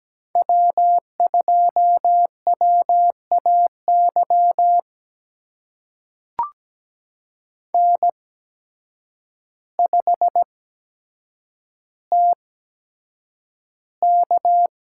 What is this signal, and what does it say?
Morse code, heard as a keyed tone, about 17 wpm, spaced as for 8 wpm, 700 Hz. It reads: W2WAY N5TK
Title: USA Call Signs - 8x Character Spacing - 17wpm